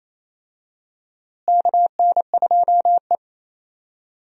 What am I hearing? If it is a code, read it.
Morse code, heard as a keyed tone, about 27 wpm, 700 Hz. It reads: KN2E
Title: USA Call Signs - 4 Characters - Repeated 3x - 28wpm